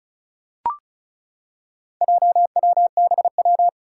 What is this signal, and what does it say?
Morse code, heard as a keyed tone, about 36 wpm, 700 Hz. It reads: JWBW
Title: Four Challenging Letter-Number Combinations 35wpm